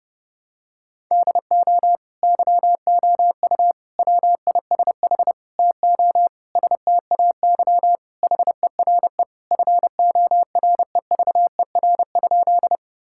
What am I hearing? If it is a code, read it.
Morse code, heard as a keyed tone, about 30 wpm, 700 Hz. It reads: DO YOU WISH TO STAY HERE FOREVER?
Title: Sentences from Top 1600 Words 30wpm